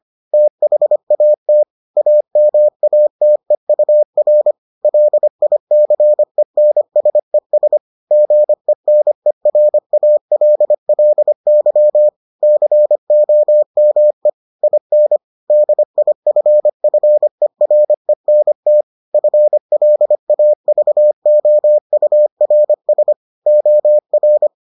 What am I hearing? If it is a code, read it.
Morse code, heard as a keyed tone, about 25 wpm, 600 Hz. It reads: THAT AMATEUR LICENSES GENERALLY COME IN DIFFERENT FLAVOURS OR